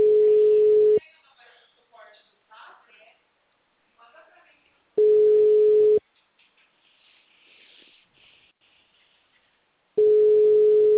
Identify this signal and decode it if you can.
Ringback tone (Europe, 425 Hz), with speech.
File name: out- -2004- - - .10144